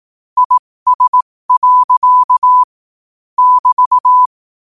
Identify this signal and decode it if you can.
Morse code, heard as a keyed tone, about 18 wpm, spaced as for 15 wpm, 1000 Hz. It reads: IS. <BT>
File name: qotdfiles / 15wpm